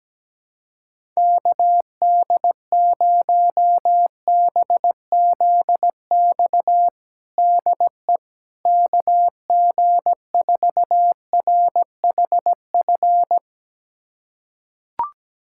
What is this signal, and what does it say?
Morse code, heard as a keyed tone, about 17 wpm, 700 Hz. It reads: KD0BZX DE KG4RHF
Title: USA Callsign DE Callsign - 6 Characters - 17wpm